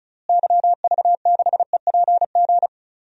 Morse code, heard as a keyed tone, about 36 wpm, 700 Hz. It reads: YV6EPG
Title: DXCC-100 Call Signs - 6 Characters - Repeated 2x - 35wpm